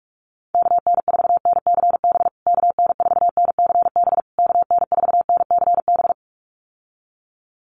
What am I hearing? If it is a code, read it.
Morse code, heard as a keyed tone, about 44 wpm, 690 Hz. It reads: KN4NCB KN4NCB KN4NCB
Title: USA Call Signs Repeated 3x 45wpm